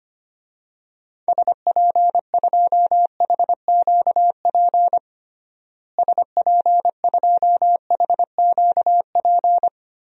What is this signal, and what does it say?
Morse code, heard as a keyed tone, about 25 wpm, 700 Hz. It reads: SP2HQP SP2HQP
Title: DXCC-100 Call Signs - 6 Characters - Repeated 3x - 25wpm